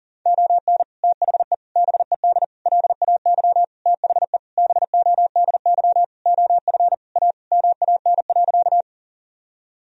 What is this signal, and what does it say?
Morse code, heard as a keyed tone, about 40 wpm, 700 Hz. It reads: ON THE BED LAY THE BODY OF A MAN.
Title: Sentences from Top 1000 Words - Review Part 2 - 40wpm